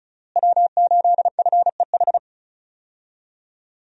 Morse code, heard as a keyed tone, about 34 wpm, 700 Hz. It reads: W8FEH
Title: USA Call Signs - 5 Characters - Repeated 2x - 35wpm